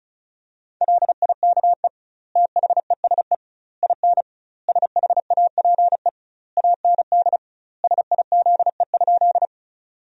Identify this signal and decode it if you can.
Morse code, heard as a keyed tone, about 36 wpm, 700 Hz. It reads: LIKE THESE IN SHAPE AND SIZE?
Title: 2x Word-Spacing - Sentences from Top 500 Words - 35wpm